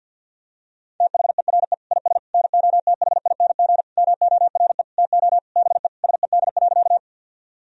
Morse code, heard as a keyed tone, about 52 wpm, 700 Hz. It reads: THERE IS NOTHING MORE TO BE SED.
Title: Sentences from Top 500 Words and Abbreviations 50wpm